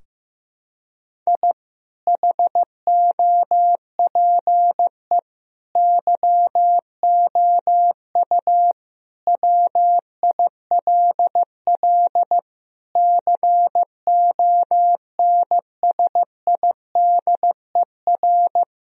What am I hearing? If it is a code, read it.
Morse code, heard as a keyed tone, about 15 wpm, 700 Hz. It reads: I HOPE YOU WILL CONSIDER
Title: Sentences from Top 1000 Words - Review Part 4 - 15wpm